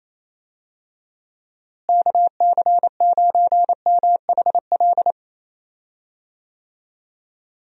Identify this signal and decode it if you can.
Morse code, heard as a keyed tone, about 28 wpm, 700 Hz. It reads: KC9MHL
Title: USA Call Signs - 6 Characters - Repeated 2x - 28wpm